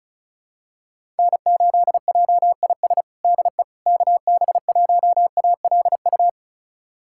Morse code, heard as a keyed tone, about 36 wpm, 700 Hz. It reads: N8JIS DE KB1ALU
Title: USA Callsign DE Callsign 35wpm